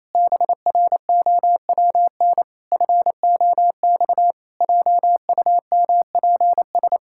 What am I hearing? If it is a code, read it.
Morse code, heard as a keyed tone, about 28 wpm, 700 Hz. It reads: BROWN FOX JUMPS
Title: Pangram Challenge v2 - 28wpm